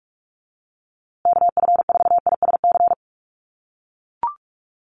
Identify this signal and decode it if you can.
Morse code, heard as a keyed tone, about 44 wpm, 700 Hz. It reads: KF4ISC
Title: USA Call Signs Repeated 3x 45wpm